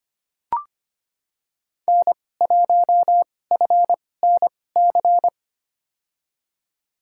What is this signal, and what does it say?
Morse code, heard as a keyed tone, about 25 wpm, spaced as for 18 wpm, 700 Hz. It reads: N1FNC
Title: USA Call Signs - 5 Characters - 2x Character Spacing - 25wpm